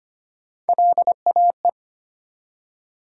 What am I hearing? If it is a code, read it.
Morse code, heard as a keyed tone, about 25 wpm, 700 Hz. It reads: LAE
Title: USA Call Sign Postfixes - 3 Letters 25wpm